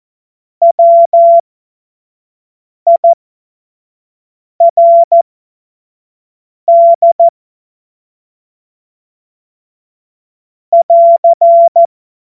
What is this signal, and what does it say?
Morse code, heard as a keyed tone, about 14 wpm, spaced as for 5 wpm, 680 Hz. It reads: WIRD <AR>
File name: Witz-29-Juli-2025-25-F70